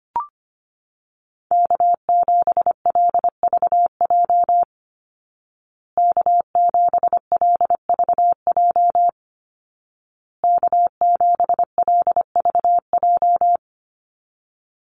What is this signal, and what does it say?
Morse code, heard as a keyed tone, about 25 wpm, 700 Hz. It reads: K7LVJ K7LVJ K7LVJ
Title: Call Signs Repeated 2x - 25wpm